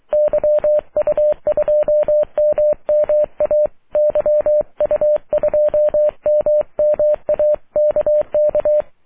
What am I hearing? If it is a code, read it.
Morse code, heard as a keyed tone, about 23 wpm, 600 Hz. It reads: YU2MMA YU2MMA KK
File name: wp / la8aja yu2mma aurora 05 aug 2011